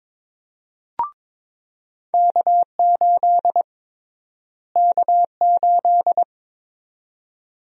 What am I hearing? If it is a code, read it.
Morse code, heard as a keyed tone, about 22 wpm, 700 Hz. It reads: K8 K8
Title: USA Call Sign Prefixes - Repeated 2x - 22wpm